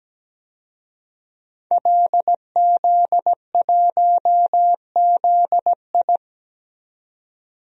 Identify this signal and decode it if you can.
Morse code, heard as a keyed tone, about 17 wpm, 700 Hz. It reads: LZ1ZI
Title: DXCC-100 Call Signs - 5 Characters - 17wpm